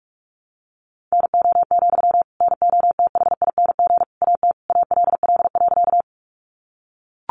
Morse code, heard as a keyed tone, about 44 wpm, 700 Hz. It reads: NO, NOTHING AT ALL.
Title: Sentences from Top 500 Words 45wpm